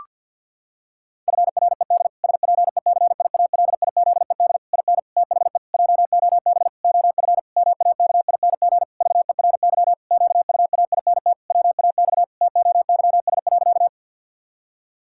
Morse code, heard as a keyed tone, about 52 wpm, 700 Hz. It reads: FRED SPECIALIZED IN THE JOB OF MAKING VERY QUAINT WAX TOYS.
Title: Pangram Challenge v2 - 50wpm